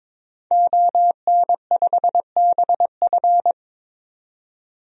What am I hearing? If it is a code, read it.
Morse code, heard as a keyed tone, about 22 wpm, 700 Hz. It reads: ON5BF
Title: DXCC-100 Call Signs - 5 Characters - 22wpm